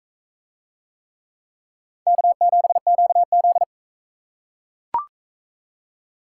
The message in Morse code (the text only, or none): K7QZ